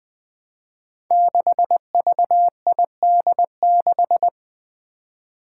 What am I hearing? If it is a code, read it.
Morse code, heard as a keyed tone, about 20 wpm, 700 Hz. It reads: 6VID6
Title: Five Challenging Letter-Number Combinations 20wpm